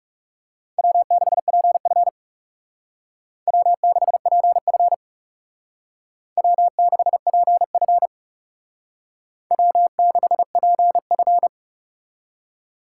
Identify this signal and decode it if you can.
Morse code, heard as a keyed tone, about 39 wpm, 700 Hz. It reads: W6PF W6PF W6PF W6PF